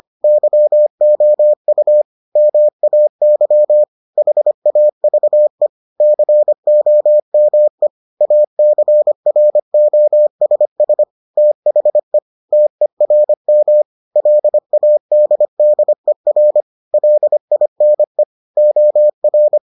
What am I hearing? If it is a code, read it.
Morse code, heard as a keyed tone, about 25 wpm, 600 Hz. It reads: YOU MAY HAVE COME ACROSS THE TERM LADDER LINE OR